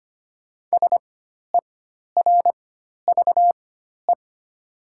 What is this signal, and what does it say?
Morse code, heard as a keyed tone, about 25 wpm, spaced as for 12 wpm, 700 Hz. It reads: SERVE
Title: 4x Character Spacing - Top 500 Words - 25wpm